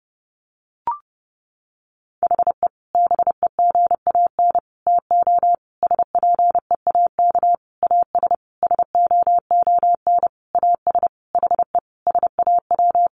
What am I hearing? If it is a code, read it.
Morse code, heard as a keyed tone, about 28 wpm, 700 Hz. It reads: HE BEGAN TO SPEAK AS SOON AS HE SAW
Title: Sentences from Top 1100 Words 30wpm